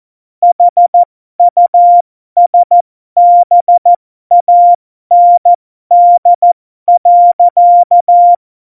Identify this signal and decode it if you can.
Morse code, heard as a keyed tone, about 14 wpm, spaced as for 12 wpm, 700 Hz. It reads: HUSBAND.
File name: Morsekurs / Witze